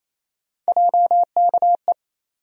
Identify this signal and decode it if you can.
Morse code, heard as a keyed tone, about 28 wpm, 700 Hz. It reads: JKE
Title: USA Call Sign Postfixes - 3 Letters 28wpm